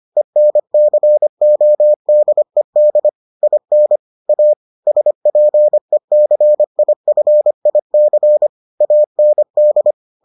Morse code, heard as a keyed tone, about 25 wpm, 600 Hz. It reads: ENCODED IN A SPECIFIC AND